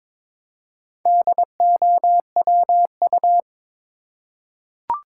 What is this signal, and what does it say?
Morse code, heard as a keyed tone, about 22 wpm, 700 Hz. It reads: DOWU
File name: practice / 22wpm / Four Letter-Number Combinations - 22wpm